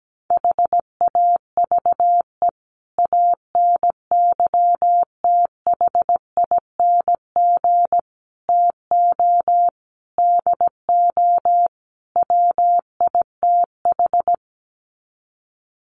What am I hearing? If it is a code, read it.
Morse code, heard as a keyed tone, about 17 wpm, 700 Hz. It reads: HAVE ANYTHING TO DO WITH
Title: Sets of 5 Words 17wpm